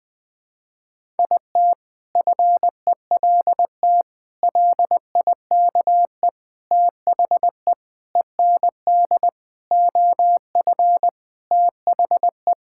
Morse code, heard as a keyed tone, about 20 wpm, 700 Hz. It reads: IT FELT LIKE THE END OF THE
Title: Sentences with 8-Word Phrases - 20wpm